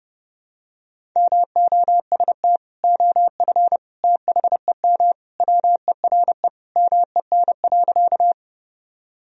Morse code, heard as a keyed tone, about 28 wpm, 700 Hz. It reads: MOST OF THEM WERE MEN.
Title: Sentences from Top 1000 Words - Review Part 1 - 30wpm